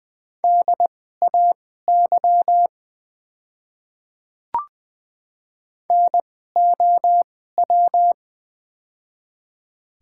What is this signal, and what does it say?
Morse code, heard as a keyed tone, about 21 wpm, spaced as for 15 wpm, 700 Hz. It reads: DAY NOW
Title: 2x Character Spacing - Top 500 Words - 20wpm